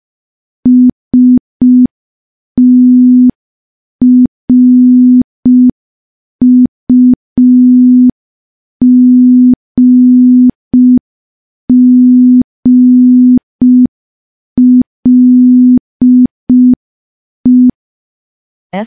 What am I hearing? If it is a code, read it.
Morse code, heard as a keyed tone, about 5 wpm, 250 Hz. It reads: STRUGGLE
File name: morse / morse-training-05WPM